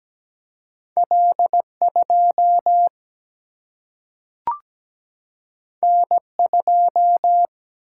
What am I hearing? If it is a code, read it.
Morse code, heard as a keyed tone, about 17 wpm, 700 Hz. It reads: L2 N2
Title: Single-Letter Single-Number 17wpm